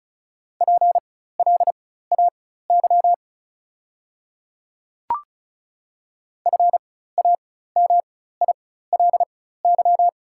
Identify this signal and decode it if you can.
Morse code, heard as a keyed tone, about 36 wpm, spaced as for 16 wpm, 700 Hz. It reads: PLAY FAMILY
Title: 4x Character Spacing - Top 500 Words - 35wpm